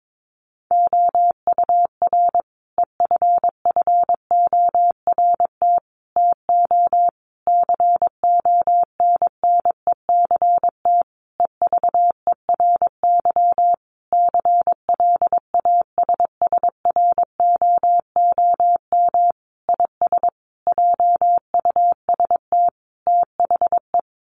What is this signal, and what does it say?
Morse code, heard as a keyed tone, about 22 wpm, 700 Hz. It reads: OUR EFFORT TO CONNECT EVERY CLASSROOM IS JUST THE
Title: Sentences from Top 1500 Words 22wpm